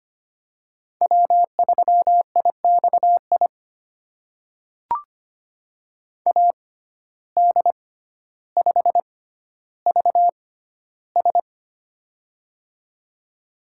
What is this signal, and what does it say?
Morse code, heard as a keyed tone, about 25 wpm, spaced as for 13 wpm, 700 Hz. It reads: W3IXI AD5VS
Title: USA Call Signs - 5 Characters - 6x Character Spacing - 25wpm